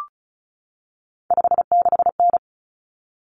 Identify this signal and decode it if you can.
Morse code, heard as a keyed tone, about 34 wpm, 700 Hz. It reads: 56N
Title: Common Cut Numbers - 3-Digit 35wpm